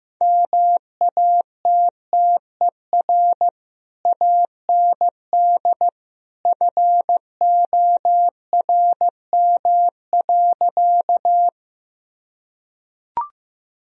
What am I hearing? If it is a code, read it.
Morse code, heard as a keyed tone, about 16 wpm, 700 Hz. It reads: MATTER AND FORM.
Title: Sentences from Top 1000 Words - Review Part 1 - 15wpm